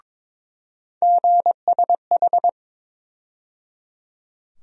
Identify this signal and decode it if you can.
Morse code, heard as a keyed tone, about 22 wpm, 700 Hz. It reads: GSH
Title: USA Call Sign Postfixes - 3 Letters 22wpm